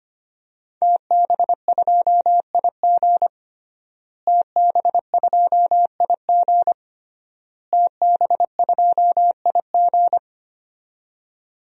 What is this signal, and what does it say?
Morse code, heard as a keyed tone, about 25 wpm, 700 Hz. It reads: TB2IG TB2IG TB2IG
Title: DXCC-100 Call Signs - 5 Characters - Repeated 3x - 25wpm